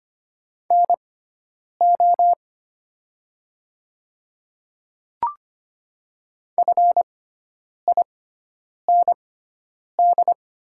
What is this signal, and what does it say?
Morse code, heard as a keyed tone, about 25 wpm, spaced as for 9 wpm, 700 Hz. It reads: NO FIND
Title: 6x Character Spacing - Top 100 Words - 25wpm